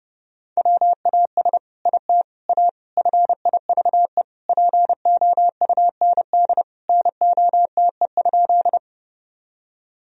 Morse code, heard as a keyed tone, about 32 wpm, 700 Hz. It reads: WAS IT A FIVE POUND NOTE?
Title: Sentences from Top 1000 Words - Review Part 2 - 30wpm